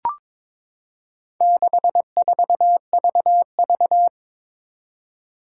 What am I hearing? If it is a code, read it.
Morse code, heard as a keyed tone, about 22 wpm, 700 Hz. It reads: 64VV
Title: Four Challenging Letter-Number Combinations 22wpm